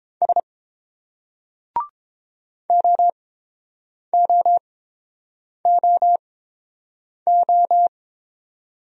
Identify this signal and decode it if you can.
Morse code, heard as a keyed tone, about 29 wpm, spaced as for 7 wpm, 690 Hz. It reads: S OOOO